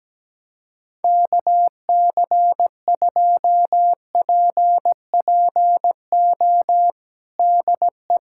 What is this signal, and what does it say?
Morse code, heard as a keyed tone, about 17 wpm, 700 Hz. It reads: KC2PPO DE
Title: USA Callsign DE Callsign - 6 Characters - 17wpm